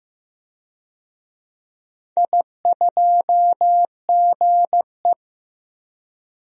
Morse code, heard as a keyed tone, about 15 wpm, 700 Hz. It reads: I2GE
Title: DXCC-100 Call Signs - 4 Characters - 15wpm